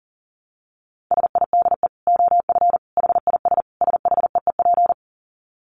Morse code, heard as a keyed tone, about 40 wpm, 700 Hz. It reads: SIDE OF HIS SHEEP
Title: Dit Challenge 40wpm